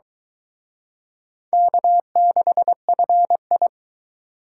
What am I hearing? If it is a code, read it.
Morse code, heard as a keyed tone, about 23 wpm, 700 Hz. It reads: K6FI